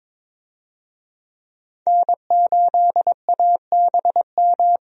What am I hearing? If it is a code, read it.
Morse code, heard as a keyed tone, about 22 wpm, 700 Hz. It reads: N8ABM
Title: USA Call Signs - 5 Characters - 22wpm